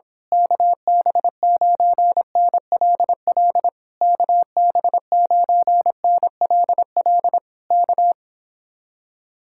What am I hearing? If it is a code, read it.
Morse code, heard as a keyed tone, about 26 wpm, 700 Hz. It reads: KB9NLL KB9NLL K
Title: Speed-Racing Common QSO Phrases Encore - 22wpm